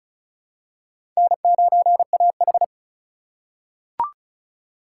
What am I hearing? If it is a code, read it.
Morse code, heard as a keyed tone, about 36 wpm, 690 Hz. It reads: N9AH
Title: USA Call Signs - 4 Characters - 2x Character Spacing - 35wpm